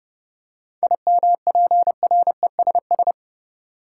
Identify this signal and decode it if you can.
Morse code, heard as a keyed tone, about 32 wpm, 700 Hz. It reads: IMPRESS
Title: Seven Letter Words - 30wpm